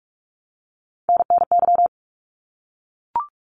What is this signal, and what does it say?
Morse code, heard as a keyed tone, about 44 wpm, 700 Hz. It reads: NNY
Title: ARRL Sections 45wpm